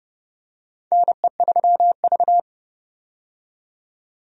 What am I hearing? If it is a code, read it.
Morse code, heard as a keyed tone, about 30 wpm, 700 Hz. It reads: NE3V
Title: USA Call Signs - 4 Characters - 30wpm